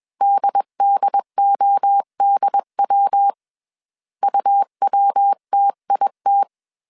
Morse code, heard as a keyed tone, about 21 wpm, 800 Hz. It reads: DDODW UWTIT